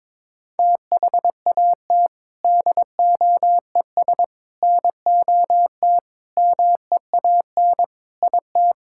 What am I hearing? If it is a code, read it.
Morse code, heard as a keyed tone, about 22 wpm, 700 Hz. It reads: THAT DOES NOT MEAN IT
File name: practice / 22wpm / Sentences with 4-Word Phrases - 22wpm